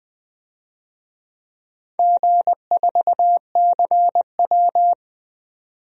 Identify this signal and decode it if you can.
Morse code, heard as a keyed tone, about 20 wpm, 700 Hz. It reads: G4CW
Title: DXCC-100 Call Signs - 4 Characters - 20wpm